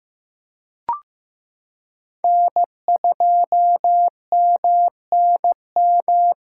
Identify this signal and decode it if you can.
Morse code, heard as a keyed tone, about 15 wpm, 700 Hz. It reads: N2MNM
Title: USA Callsign DE Callsign - 5 Characters - 15wpm